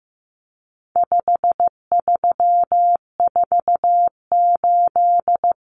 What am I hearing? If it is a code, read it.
Morse code, heard as a keyed tone, about 15 wpm, 700 Hz. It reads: 5348
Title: Common Cut Numbers - 4-Digit 15wpm